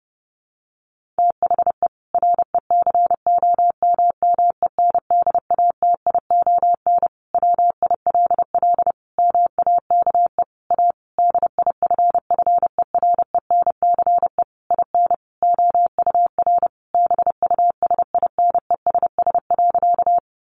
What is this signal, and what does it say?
Morse code, heard as a keyed tone, about 30 wpm, 700 Hz. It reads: THE RECOMMENDATION WILL MAKE A DIFFERENCE IN OUR BUSINESS.
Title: Sentences from Top 1400 Words 30wpm